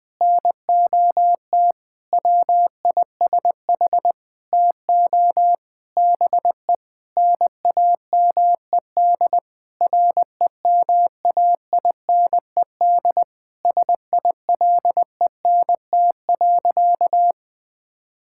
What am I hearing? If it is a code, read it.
Morse code, heard as a keyed tone, about 21 wpm, 700 Hz. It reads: NOT WISH TO BE NAMED REMAINED SILENT.
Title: Sentences with 7-Word Phrases - 20wpm